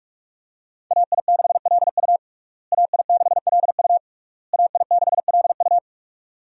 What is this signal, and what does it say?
Morse code, heard as a keyed tone, about 46 wpm, 700 Hz. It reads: AI6LU AI6LU AI6LU
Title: USA Call Signs - 5 Characters - Repeated 3x - 45wpm